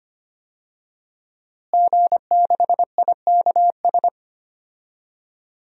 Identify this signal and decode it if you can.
Morse code, heard as a keyed tone, about 25 wpm, 700 Hz. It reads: G6IKS
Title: DXCC-100 Call Signs - 5 Characters - Repeated 2x - 25wpm